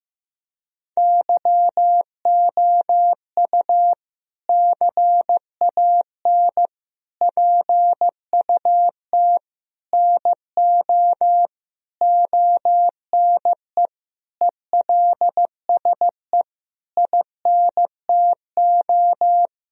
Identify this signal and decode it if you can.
Morse code, heard as a keyed tone, about 15 wpm, 700 Hz. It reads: YOU CAN PUT NO ONE ELSE INTO
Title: Sentences from Top 1000 Words - Review Part 2 - 15wpm